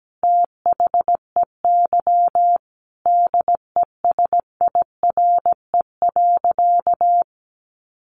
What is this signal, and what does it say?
Morse code, heard as a keyed tone, about 17 wpm, 700 Hz. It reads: THEY DESIRE.
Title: Sentences from Top 1100 Words 17wpm